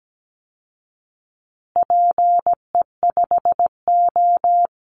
Morse code, heard as a keyed tone, about 17 wpm, 700 Hz. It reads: PE5O
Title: Call Signs Repeated 3x - 17wpm